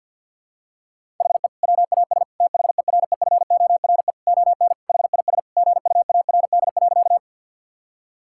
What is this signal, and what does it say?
Morse code, heard as a keyed tone, about 53 wpm, 700 Hz. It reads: HE WAS THEREFORE ON HIS GUARD.
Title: Sentences from Top 1000 Words - Review Part 5 - 50wpm